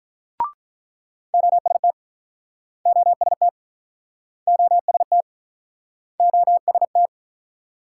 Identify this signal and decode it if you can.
Morse code, heard as a keyed tone, about 47 wpm, 700 Hz. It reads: OST OST OST OST